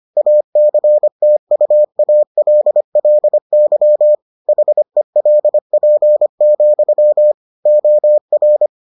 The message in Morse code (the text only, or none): ACTUALLY HELP, OR